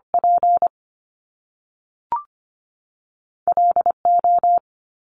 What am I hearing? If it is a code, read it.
Morse code, heard as a keyed tone, about 25 wpm, 700 Hz. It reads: P L O
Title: Top Two Letter Combinations in Words 25wpm